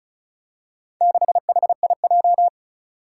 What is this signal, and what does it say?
Morse code, heard as a keyed tone, about 36 wpm, 700 Hz. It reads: 6HIJ
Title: Four Challenging Letter-Number Combinations 35wpm